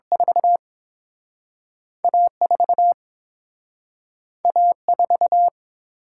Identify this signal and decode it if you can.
Morse code, heard as a keyed tone, about 31 wpm, 700 Hz. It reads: 4 A4 A4